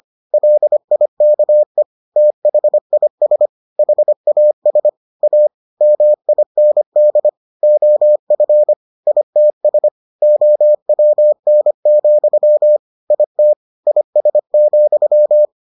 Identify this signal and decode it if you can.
Morse code, heard as a keyed tone, about 25 wpm, 600 Hz. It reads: LIKE THIS HAS A MIND OF ITS OWN, IT IS,